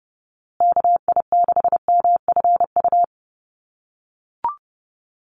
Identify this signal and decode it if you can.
Morse code, heard as a keyed tone, about 30 wpm, 700 Hz. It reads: KI6MFU
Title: USA Call Signs Repeated 3x 30wpm